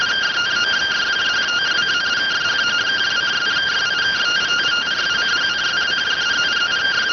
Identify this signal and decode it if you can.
Two-tone teleprinter signal, about 170 Hz shift.